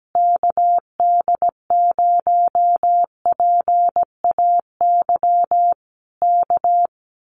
Morse code, heard as a keyed tone, about 17 wpm, 700 Hz. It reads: KD0PAY K
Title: Common QSO Phrases Encore 17wpm